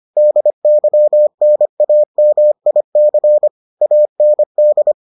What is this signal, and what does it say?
Morse code, heard as a keyed tone, about 25 wpm, 600 Hz. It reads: DYNAMIC AND